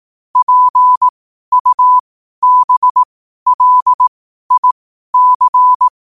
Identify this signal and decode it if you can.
Morse code, heard as a keyed tone, about 18 wpm, spaced as for 12 wpm, 1000 Hz. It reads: PUBLIC
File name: qotdfiles / 12wpm